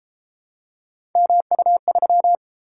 Morse code, heard as a keyed tone, about 31 wpm, 700 Hz. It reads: MU3